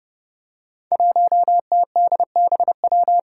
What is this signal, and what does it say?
Morse code, heard as a keyed tone, about 30 wpm, 700 Hz. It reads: 1TDBW
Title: Five Challenging Letter-Number Combinations 30wpm